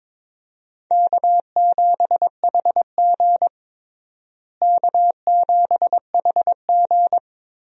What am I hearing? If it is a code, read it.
Morse code, heard as a keyed tone, about 23 wpm, 700 Hz. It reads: K7HG K7HG
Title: USA Call Signs - 4 Characters - Repeated 2x - 22wpm